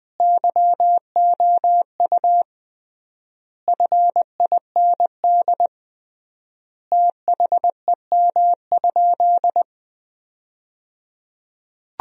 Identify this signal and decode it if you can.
Morse code, heard as a keyed tone, about 20 wpm, 700 Hz. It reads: YOU FIND THEM?
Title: 3x Word-Spacing - Sentences from Top 200 Words - 20wpm